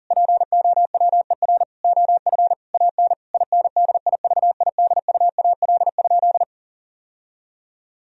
Morse code, heard as a keyed tone, about 37 wpm, 700 Hz. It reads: POWER OF AN INDIVIDUAL?
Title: Sentences from Top 1000 Words - Review Part 4 - 40wpm